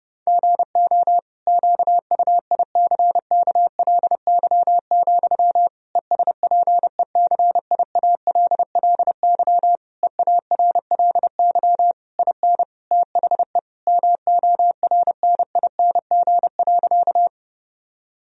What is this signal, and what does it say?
Morse code, heard as a keyed tone, about 30 wpm, 700 Hz. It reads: GO QUICKLY, ESPECIALLY EARLY IN THE MORNING.
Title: Sentences from Top 1000 Words - Review Part 4 - 30wpm